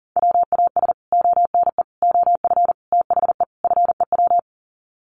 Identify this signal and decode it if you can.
Morse code, heard as a keyed tone, about 40 wpm, 700 Hz. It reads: WAS ONE OF THE FEW
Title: Sets of 5 Words Encore 40wpm